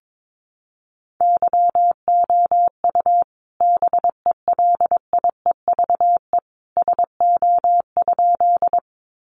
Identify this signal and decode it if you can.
Morse code, heard as a keyed tone, about 22 wpm, 700 Hz. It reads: YOU BELIEVE SO?
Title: Sentences from Top 1000 Words 22wpm